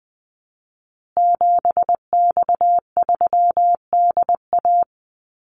Morse code, heard as a keyed tone, about 20 wpm, 700 Hz. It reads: 7X3DA
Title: Call Signs Repeated 3x - 20wpm